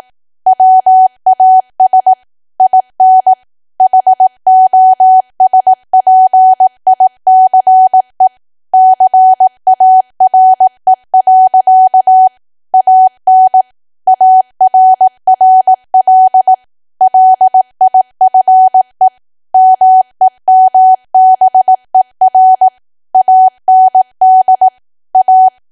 Morse code, heard as a keyed tone, about 18 wpm, 750 Hz. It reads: WAS IN HOSPICE CARE. AN ARRL LIFE MEMBER AND A
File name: Archive / 18 WPM / 18WPM